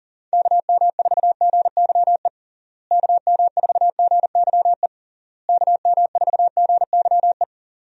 Morse code, heard as a keyed tone, about 37 wpm, 700 Hz. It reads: KM4GYE KM4GYE KM4GYE
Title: USA Call Signs - 6 Characters - Repeated 3x - 40wpm